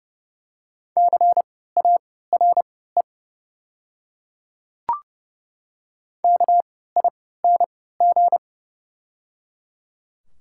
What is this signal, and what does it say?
Morse code, heard as a keyed tone, about 29 wpm, spaced as for 17 wpm, 700 Hz. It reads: CARE KING
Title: 3x Character Spacing - Top 500 Words - 30wpm